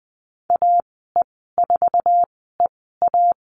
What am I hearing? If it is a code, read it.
Morse code, heard as a keyed tone, about 20 wpm, spaced as for 14 wpm, 700 Hz. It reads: AE4EA
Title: 069 Lesson 12g - Letter d - Call Signs 20wpm